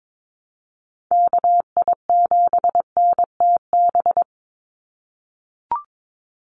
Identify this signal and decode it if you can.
Morse code, heard as a keyed tone, about 22 wpm, 700 Hz. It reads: KI7NTB
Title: USA Call Signs 22wpm